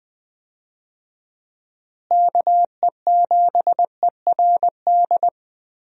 Morse code, heard as a keyed tone, about 20 wpm, 700 Hz. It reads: KE7ERD
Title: USA Call Signs - 6 Characters - Repeated 2x - 20wpm